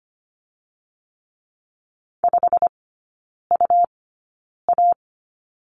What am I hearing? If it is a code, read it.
Morse code, heard as a keyed tone, about 25 wpm, spaced as for 9 wpm, 700 Hz. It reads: 5UA
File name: Lesson 17e - Letter u - Three Characters 25wpm